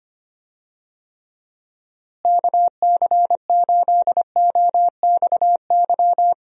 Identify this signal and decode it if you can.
Morse code, heard as a keyed tone, about 25 wpm, 700 Hz. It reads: KC8OXY
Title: USA Call Signs - 6 Characters - Repeated 3x - 25wpm